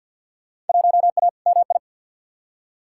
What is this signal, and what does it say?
Morse code, heard as a keyed tone, about 48 wpm, 700 Hz. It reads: 1A MI
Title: Field Day Exchanges - 50wpm